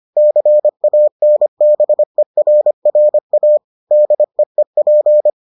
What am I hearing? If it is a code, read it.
Morse code, heard as a keyed tone, about 25 wpm, 600 Hz. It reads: CANBERRA DEEP